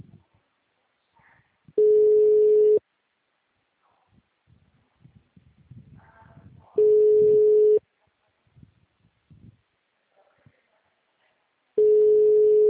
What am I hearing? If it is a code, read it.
Ringback tone (Europe, 425 Hz).
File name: out- -2019- - - .10